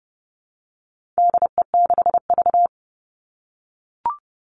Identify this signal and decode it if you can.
Morse code, heard as a keyed tone, about 30 wpm, 700 Hz. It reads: DE6V
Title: All Cut Numbers - 4-Digit 30wpm